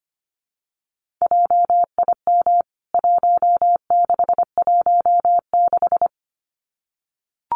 Morse code, heard as a keyed tone, about 25 wpm, 700 Hz. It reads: JIM 1616
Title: CWT Exchange 25wpm